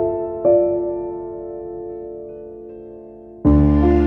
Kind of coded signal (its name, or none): telephone call progress tone